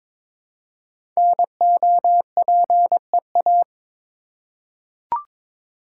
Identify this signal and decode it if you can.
Morse code, heard as a keyed tone, about 22 wpm, 700 Hz. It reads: NOPEA
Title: Code Groups - 5 Letters - 22wpm